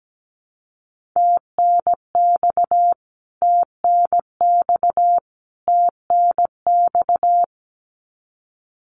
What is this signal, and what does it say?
Morse code, heard as a keyed tone, about 17 wpm, 700 Hz. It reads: TNX TNX TNX
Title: Abbreviations Repeated 3x - 17wpm